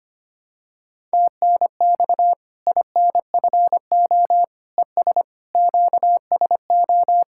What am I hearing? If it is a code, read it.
Morse code, heard as a keyed tone, about 25 wpm, 700 Hz. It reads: TNX INFO ES QSO
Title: K1USN SST QSO Practice 25wpm